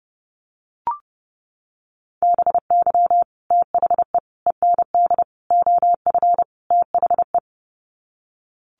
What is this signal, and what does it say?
Morse code, heard as a keyed tone, about 30 wpm, 700 Hz. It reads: BY THE END OF THE
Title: Sets of 5 Words 30wpm